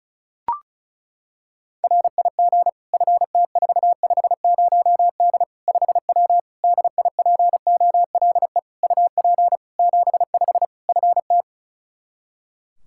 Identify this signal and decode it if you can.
Morse code, heard as a keyed tone, about 36 wpm, 700 Hz. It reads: RIG FT450D 5W DIPOLE UP 75 FT
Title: Practice QSO - Exchange 2 - Info - 35wpm